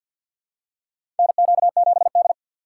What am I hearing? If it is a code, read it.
Morse code, heard as a keyed tone, about 47 wpm, 700 Hz. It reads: NQ7D
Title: USA Call Signs - 4 Characters - 3x Character Spacing - 50wpm